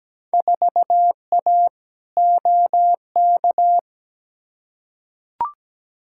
Morse code, heard as a keyed tone, about 17 wpm, 700 Hz. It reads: 4A OK
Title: Field Day Exchanges - 17wpm